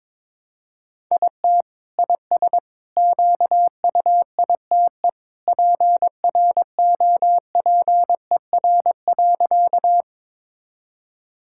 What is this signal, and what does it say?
Morse code, heard as a keyed tone, about 21 wpm, 700 Hz. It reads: IT IS QUITE PROPER.
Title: Sentences from Top 1000 Words - Review Part 5 - 22wpm